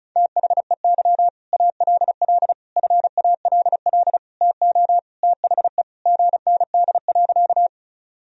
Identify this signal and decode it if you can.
Morse code, heard as a keyed tone, about 36 wpm, 700 Hz. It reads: THEY ALL FALL TO THE GND.
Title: Sentences from Top 500 Words and Abbreviations 35wpm